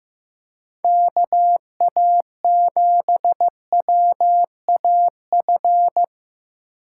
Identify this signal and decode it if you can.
Morse code, heard as a keyed tone, about 18 wpm, 700 Hz. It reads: KA7WAF
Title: Speed Racing - USA 2x3 Call Signs 15wpm